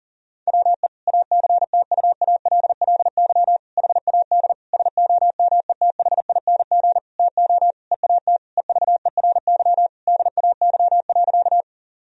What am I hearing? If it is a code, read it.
Morse code, heard as a keyed tone, about 40 wpm, 700 Hz. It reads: WE ACTUALLY HAD SOMETHING TO EAT EVERY DAY.
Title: Sentences from Top 1000 Words - Review Part 4 - 40wpm